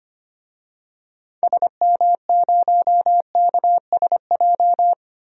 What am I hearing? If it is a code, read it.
Morse code, heard as a keyed tone, about 25 wpm, 700 Hz. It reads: SM0KSJ
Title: DXCC-100 Call Signs - 6 Characters - 25wpm